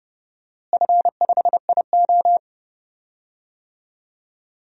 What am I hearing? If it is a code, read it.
Morse code, heard as a keyed tone, about 30 wpm, 700 Hz. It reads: F5IO
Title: DXCC-100 Call Signs - 4 Characters - Repeated 3x - 30wpm